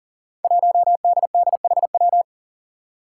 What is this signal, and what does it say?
Morse code, heard as a keyed tone, about 40 wpm, 700 Hz. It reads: 1DDHW
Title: Five Challenging Letter-Number Combinations 40wpm